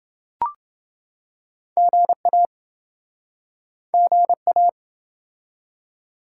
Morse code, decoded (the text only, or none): GA GA